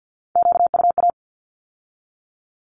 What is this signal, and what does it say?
Morse code, heard as a keyed tone, about 50 wpm, 700 Hz. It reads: QUA
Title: Q-Codes with Spoken Meaning - 50wpm